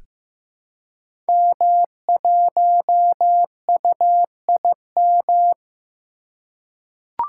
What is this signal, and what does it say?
Morse code, heard as a keyed tone, about 15 wpm, 700 Hz. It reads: M1UIM
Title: Code Groups - 5 Letter:Numbers - 15wpm